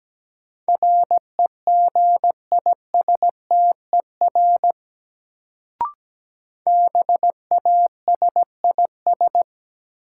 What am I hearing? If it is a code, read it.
Morse code, heard as a keyed tone, about 17 wpm, 700 Hz. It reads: REGISTER BASIS
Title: Sentences from Top 1200 Words - New Word Drill Alt - 17wpm